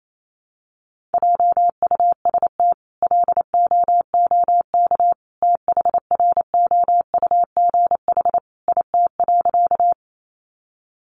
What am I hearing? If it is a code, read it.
Morse code, heard as a keyed tone, about 28 wpm, 700 Hz. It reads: JUST LOOK THROUGH IT.
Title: Sentences from Top 200 Words 28wpm